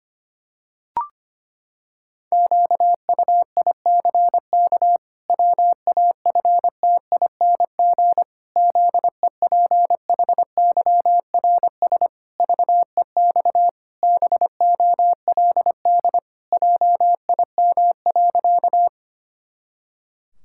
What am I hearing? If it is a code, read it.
Morse code, heard as a keyed tone, about 25 wpm, 700 Hz. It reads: QUICK WAFTING ZEPHYRS VEX BOLD JIM.
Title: Pangram Challenge v2 - 25wpm